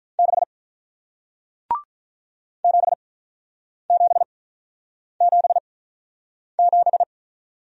Morse code, heard as a keyed tone, about 48 wpm, spaced as for 9 wpm, 700 Hz. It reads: 6 7777